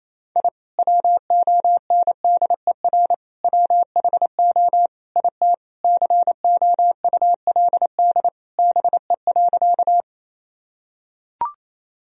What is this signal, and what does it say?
Morse code, heard as a keyed tone, about 28 wpm, 700 Hz. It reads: I WONDER WHO IT COULD BE.
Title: Sentences from Top 1000 Words - Review Part 3 - 28wpm